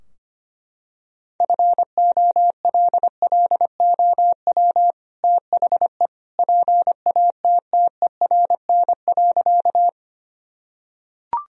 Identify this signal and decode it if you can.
Morse code, heard as a keyed tone, about 25 wpm, 700 Hz. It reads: FOLLOW THE PATTERN.
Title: Sentences from Top 1000 Words - Review Part 2 - 25wpm